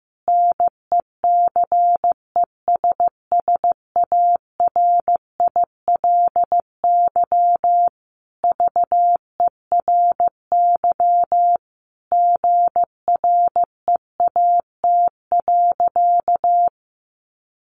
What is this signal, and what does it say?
Morse code, heard as a keyed tone, about 15 wpm, 700 Hz. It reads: NECESSARILY VERY GREAT.
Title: Sentences from Top 1400 Words 15wpm